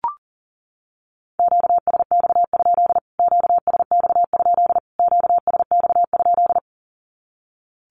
Morse code, heard as a keyed tone, about 40 wpm, 700 Hz. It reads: QSX? QSX? QSX?
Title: Q-Codes Repeated 3x 40wpm